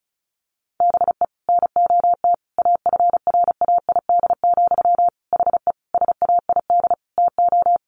Morse code, heard as a keyed tone, about 34 wpm, 700 Hz. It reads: BE NOT AFRAID, HE SAID TO
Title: Sentences from Top 1500 Words 35wpm